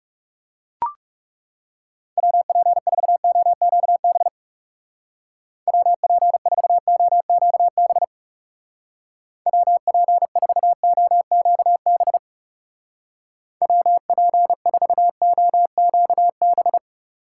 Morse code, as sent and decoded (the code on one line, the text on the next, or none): .-- .--. ....- --- --.- -... / .-- .--. ....- --- --.- -... / .-- .--. ....- --- --.- -... / .-- .--. ....- --- --.- -...
WP4OQB WP4OQB WP4OQB WP4OQB